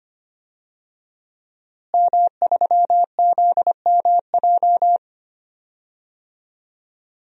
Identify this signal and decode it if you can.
Morse code, heard as a keyed tone, about 25 wpm, 700 Hz. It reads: M3ZMJ
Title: DXCC-100 Call Signs - 5 Characters - Repeated 3x - 25wpm